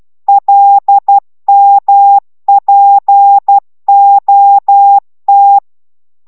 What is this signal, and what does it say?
Morse code, heard as a keyed tone, about 12 wpm, 800 Hz. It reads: LMPOT